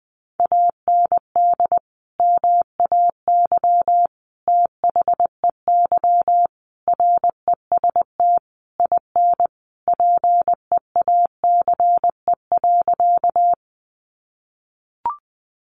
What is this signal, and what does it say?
Morse code, heard as a keyed tone, about 20 wpm, 700 Hz. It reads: AND MAY THEY REST IN PEACE.
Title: Sentences from Top 1500 Words 20wpm